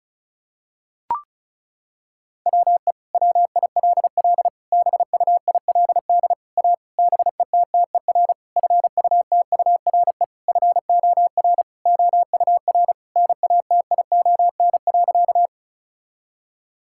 Morse code, heard as a keyed tone, about 34 wpm, 700 Hz. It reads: WE WILL BUILD A BETTER FUTURE FOR OUR NATION.
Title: Sentences from Top 1000 Words - Review Part 6 - 35wpm